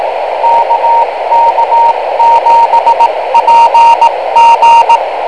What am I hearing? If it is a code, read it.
Morse code, heard as a keyed tone, about 18 wpm, spaced as for 15 wpm, 930 Hz, with radio noise. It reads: KK7PG
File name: kl7fdq / beacons / kk7pg